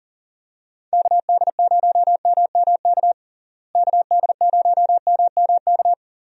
Morse code, heard as a keyed tone, about 40 wpm, 700 Hz. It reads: KD0MMK KD0MMK
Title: USA Call Signs - 6 Characters - Repeated 3x - 40wpm